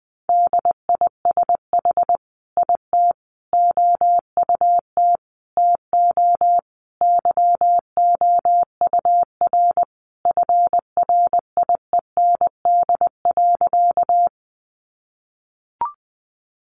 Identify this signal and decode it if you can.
Morse code, heard as a keyed tone, about 20 wpm, 700 Hz. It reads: DISH IT OUT TO YOUR FRIEND.
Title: Sentences from Top 1300 Words 20wpm